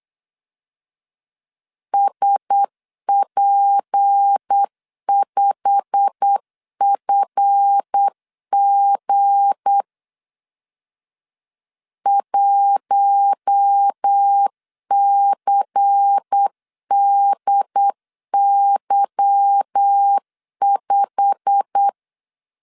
Morse code, heard as a keyed tone, about 8 wpm, 800 Hz. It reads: SP5FG 1CDY5